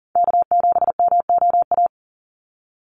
Morse code, heard as a keyed tone, about 41 wpm, 700 Hz. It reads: K7MOA
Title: Call Signs - 40wpm